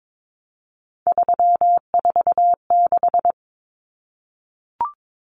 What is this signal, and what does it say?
Morse code, heard as a keyed tone, about 22 wpm, 700 Hz. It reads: 346
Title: Common Cut Numbers - 3-Digit 22wpm